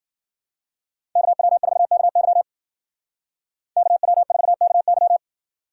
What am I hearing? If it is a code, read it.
Morse code, heard as a keyed tone, about 49 wpm, 700 Hz. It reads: KW4KY KW4KY
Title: Speed Racing - USA Call Signs 40wpm